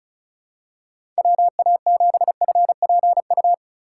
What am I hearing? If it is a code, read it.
Morse code, heard as a keyed tone, about 34 wpm, 700 Hz. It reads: WA7FPU
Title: USA Call Signs - 6 Characters - 3x Character Spacing - 35wpm